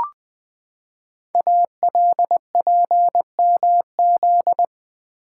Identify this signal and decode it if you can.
Morse code, heard as a keyed tone, about 20 wpm, 700 Hz. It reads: ALPMZ
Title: Code Groups - 5 Letter:Numbers - 20wpm